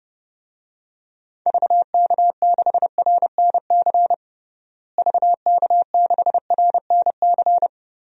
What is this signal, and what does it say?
Morse code, heard as a keyed tone, about 32 wpm, 700 Hz. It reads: VK6RNC VK6RNC
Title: DXCC-100 Call Signs - 6 Characters - Repeated 3x - 30wpm